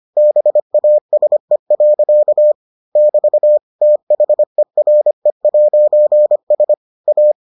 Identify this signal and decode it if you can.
Morse code, heard as a keyed tone, about 25 wpm, 600 Hz. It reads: BASE. <BT> THERE'S A